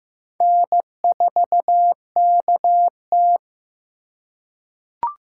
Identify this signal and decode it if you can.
Morse code, heard as a keyed tone, about 15 wpm, 700 Hz. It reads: N4KT
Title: USA Call Signs - 4 Characters - 6x Character Spacing - 15wpm